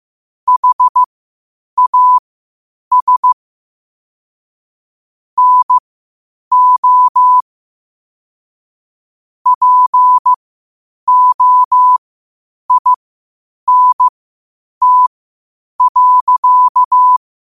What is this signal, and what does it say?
Morse code, heard as a keyed tone, about 15 wpm, spaced as for 8 wpm, 1000 Hz. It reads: HAS NO POINT.
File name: qotdfiles / 8wpm